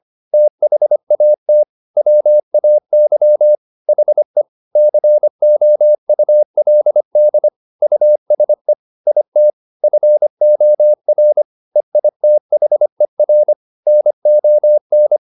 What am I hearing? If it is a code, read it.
Morse code, heard as a keyed tone, about 25 wpm, 600 Hz. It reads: THAT WAY HE COULD USE IT FOR EITHER NON